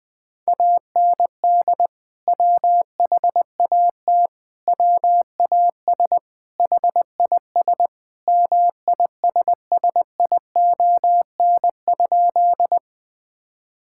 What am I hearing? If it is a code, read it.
Morse code, heard as a keyed tone, about 20 wpm, 700 Hz. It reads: AND WHAT WAS HIS MISSION?
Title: Sentences from Top 1600 Words 20wpm